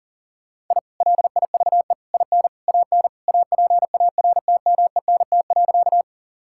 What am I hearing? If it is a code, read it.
Morse code, heard as a keyed tone, about 40 wpm, 700 Hz. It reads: I LIVE IN AN APARTMENT.
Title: Sentences from Top 1000 Words - Review Part 6 - 40wpm